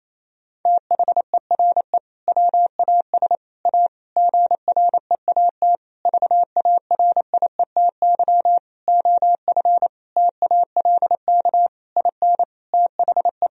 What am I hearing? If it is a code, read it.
Morse code, heard as a keyed tone, about 28 wpm, 700 Hz. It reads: THERE WAS A GREAT VARIETY OF TALK IN THE
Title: Sentences from Top 1000 Words - Review Part 4 - 28wpm